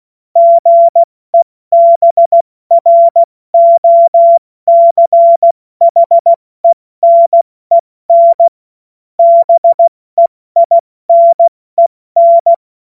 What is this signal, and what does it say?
Morse code, heard as a keyed tone, about 16 wpm, 680 Hz. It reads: GEBROCHENEN BEINEN
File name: Morsekurs / Witze / Witz 21. September 2021, 70 F80